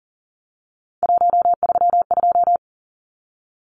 Morse code, heard as a keyed tone, about 40 wpm, 700 Hz. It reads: 132
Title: Numbers - 3-Digit 40wpm